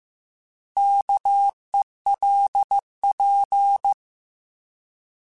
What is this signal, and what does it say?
Morse code, heard as a keyed tone, about 15 wpm, 800 Hz. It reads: KELP
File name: Alphabetical / Alphabetical mp3s / Ks @13